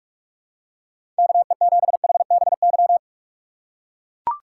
Morse code, heard as a keyed tone, about 44 wpm, 700 Hz. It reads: KE7HBY
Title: USA Call Signs - 6 Characters - 6x Character Spacing - 45wpm